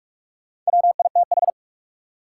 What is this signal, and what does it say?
Morse code, heard as a keyed tone, about 47 wpm, 700 Hz. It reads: WITH
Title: Sentences from Top 100 Words - New Word Drill - 45wpm